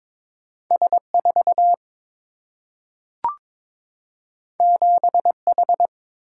Morse code, heard as a keyed tone, about 22 wpm, 700 Hz. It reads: S4 7H
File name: practice / 22wpm / Two Letter-Number Combinations - 22wpm